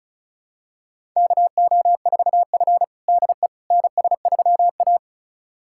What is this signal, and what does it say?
Morse code, heard as a keyed tone, about 36 wpm, 700 Hz. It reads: KO4F DE NS3A
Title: USA Callsign DE Callsign - 4 Characters - 35wpm